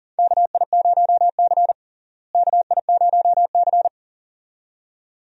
Morse code, heard as a keyed tone, about 40 wpm, 700 Hz. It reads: KI0C KI0C
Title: USA Call Signs - 4 Characters - Repeated 2x - 40wpm